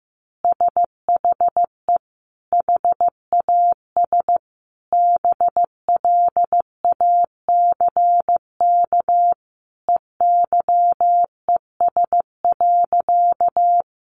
Morse code, heard as a keyed tone, about 15 wpm, 700 Hz. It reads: SHE HAS BLACK EYES.
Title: Easy Phrases 15wpm